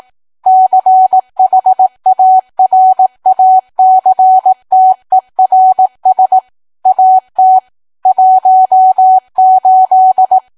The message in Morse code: CHARACTERS AT 18